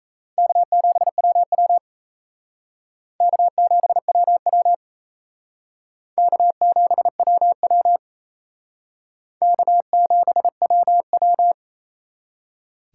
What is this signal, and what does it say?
Morse code, heard as a keyed tone, about 35 wpm, 700 Hz. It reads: K7WW K7WW K7WW K7WW